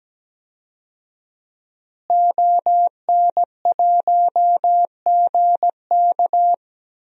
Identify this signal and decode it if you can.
Morse code, heard as a keyed tone, about 17 wpm, 700 Hz. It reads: ON1GK
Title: DXCC-100 Call Signs - 5 Characters - Repeated 2x - 17wpm